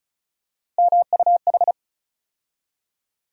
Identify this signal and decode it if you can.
Morse code, heard as a keyed tone, about 34 wpm, 700 Hz. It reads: MUH
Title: USA Call Sign Postfixes - 3 Letters 35wpm